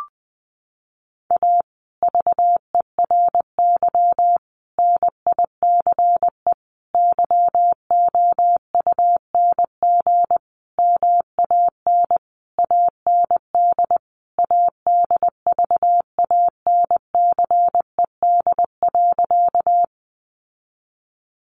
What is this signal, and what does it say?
Morse code, heard as a keyed tone, about 20 wpm, 700 Hz. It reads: A VERY NICE YOUNG MAN AND ADVANCED.
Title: Sentences from Top 1200 Words 20wpm